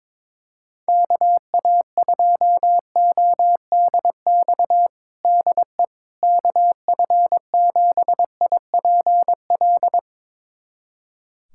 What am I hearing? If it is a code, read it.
Morse code, heard as a keyed tone, about 22 wpm, 700 Hz. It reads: KA2ODX DE KF7IPL
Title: USA Callsign DE Callsign - 6 Characters - 22wpm